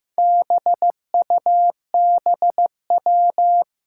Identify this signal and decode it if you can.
Morse code, heard as a keyed tone, about 15 wpm, 700 Hz. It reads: BUBW
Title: Four Challenging Letter-Number Combinations 15wpm